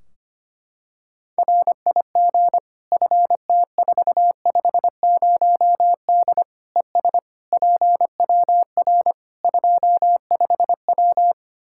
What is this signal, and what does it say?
Morse code, heard as a keyed tone, about 25 wpm, 700 Hz. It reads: RIG FT450D ES PWR 25W
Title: K1USN SST QSO Practice 25wpm